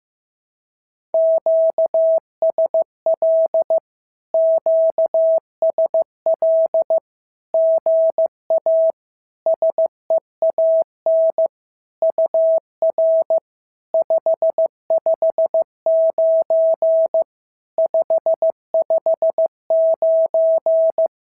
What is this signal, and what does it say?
Morse code, heard as a keyed tone, about 15 wpm, 650 Hz. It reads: QSL QSL GA SEAN UR 559 559
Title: SOTA QSO Practice 15wpm